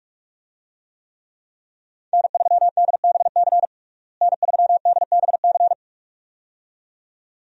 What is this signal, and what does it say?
Morse code, heard as a keyed tone, about 42 wpm, 700 Hz. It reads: N3DBC N3DBC
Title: USA Call Signs - 5 Characters - Repeated 2x - 45wpm